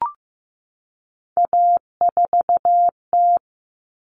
Morse code, heard as a keyed tone, about 15 wpm, 700 Hz. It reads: A4T
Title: All Cut Numbers - 3-Digit 15wpm